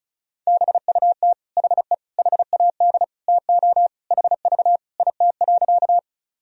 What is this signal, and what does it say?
Morse code, heard as a keyed tone, about 34 wpm, 700 Hz. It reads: BUT HE HAD TO HV IT.
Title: Sentences from Top 100 Words and Abbreviations 35wpm